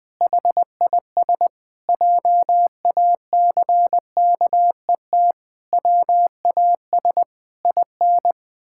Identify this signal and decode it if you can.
Morse code, heard as a keyed tone, about 20 wpm, 700 Hz. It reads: HIS JACKET WAS IN
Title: Pangram Challenge v2 - 20wpm